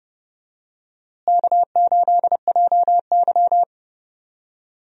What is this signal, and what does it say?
Morse code, heard as a keyed tone, about 30 wpm, 700 Hz. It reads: K8JY
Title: USA Call Signs - 4 Characters - Repeated 2x - 30wpm